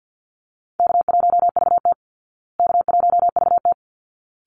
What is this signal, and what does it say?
Morse code, heard as a keyed tone, about 48 wpm, 700 Hz. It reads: K1VT K1VT
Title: Easy Call Signs Repeated 2x - 50wpm